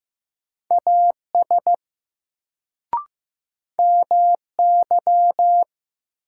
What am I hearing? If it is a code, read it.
Morse code, heard as a keyed tone, about 15 wpm, 700 Hz. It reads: AS MY
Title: Sets of 2 Words - New Word Drill - 15wpm